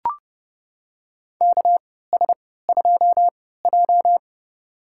Morse code, heard as a keyed tone, about 30 wpm, spaced as for 17 wpm, 700 Hz. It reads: KS2J
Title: USA Call Signs - 4 Characters - 3x Character Spacing - 30wpm